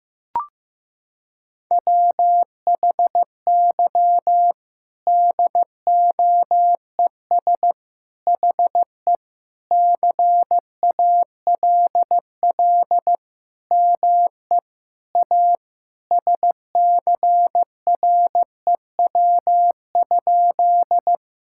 Morse code, heard as a keyed tone, about 15 wpm, 700 Hz. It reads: WHY DOES HE CALL ME A SCREW?
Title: Sentences from Top 1600 Words 15wpm